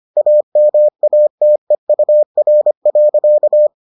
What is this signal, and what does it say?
Morse code, heard as a keyed tone, about 25 wpm, 600 Hz. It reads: AMATEUR.